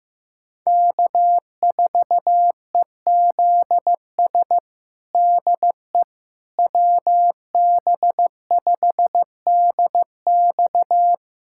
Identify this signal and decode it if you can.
Morse code, heard as a keyed tone, about 16 wpm, 700 Hz. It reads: K4EZS DE WB5DX
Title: USA Callsign DE Callsign - 5 Characters - 15wpm